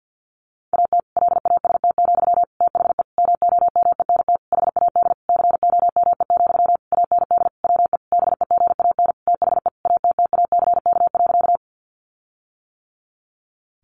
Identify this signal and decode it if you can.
Morse code, heard as a keyed tone, about 50 wpm, 700 Hz. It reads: AT LAST, THE MOMENT HAD COME, AND WE BEGAN THE ATTACK.
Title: Sentences from Top 1500 Words 50wpm